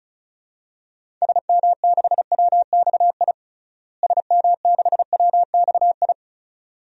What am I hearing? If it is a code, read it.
Morse code, heard as a keyed tone, about 36 wpm, 700 Hz. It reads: SM6WXI SM6WXI
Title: DXCC-100 Call Signs - 6 Characters - Repeated 2x - 35wpm